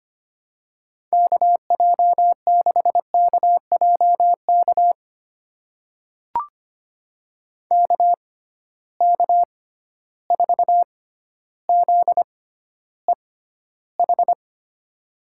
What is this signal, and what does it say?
Morse code, heard as a keyed tone, about 25 wpm, spaced as for 13 wpm, 700 Hz. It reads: KJ6KJK KK4ZEH
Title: USA Call Signs - 6 Characters - 6x Character Spacing - 25wpm